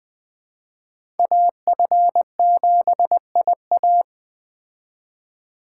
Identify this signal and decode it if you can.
Morse code, heard as a keyed tone, about 20 wpm, 700 Hz. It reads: AF7IA